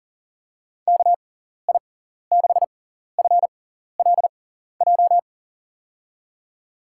Morse code, heard as a keyed tone, about 40 wpm, spaced as for 14 wpm, 700 Hz. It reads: KI6FLJ
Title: USA Call Signs - 6x Character Spacing - 40wpm